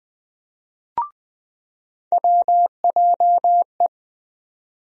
Morse code, heard as a keyed tone, about 20 wpm, 700 Hz. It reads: WJE
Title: USA Call Sign Postfixes - 3 Letters 20wpm